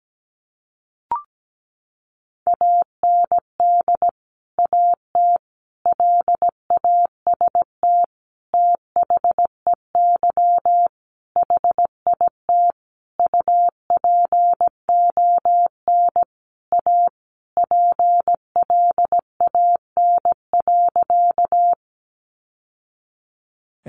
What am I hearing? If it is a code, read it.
Morse code, heard as a keyed tone, about 17 wpm, 700 Hz. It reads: AND AT LAST THEY HIT UPON A PLAN.
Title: Sentences from Top 1200 Words 17wpm